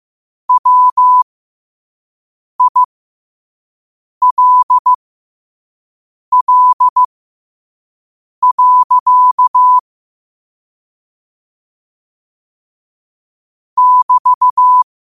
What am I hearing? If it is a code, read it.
Morse code, heard as a keyed tone, about 15 wpm, spaced as for 5 wpm, 1000 Hz. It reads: WILL. <BT>
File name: qotdfiles / 5wpm